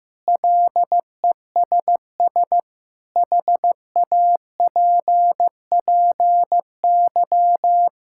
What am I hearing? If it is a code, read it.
Morse code, heard as a keyed tone, about 15 wpm, 700 Hz. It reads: LESS HAPPY
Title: Sentences from Top 1000 Words - Review Part 5 - 15wpm